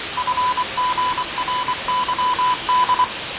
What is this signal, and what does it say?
Morse code, heard as a keyed tone, about 24 wpm, 1000 Hz, with noise. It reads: FGRYD
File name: fav22-6824usb-20251029-1420z-voice-test-win10